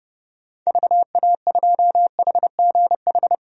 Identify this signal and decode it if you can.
Morse code, heard as a keyed tone, about 32 wpm, 700 Hz. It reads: VA2HGH
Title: DXCC-100 Call Signs - 6 Characters - Repeated 2x - 30wpm